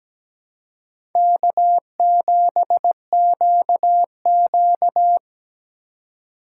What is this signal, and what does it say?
Morse code, heard as a keyed tone, about 17 wpm, 700 Hz. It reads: K7QQ
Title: USA Call Signs - 4 Characters - 8x Character Spacing - 17wpm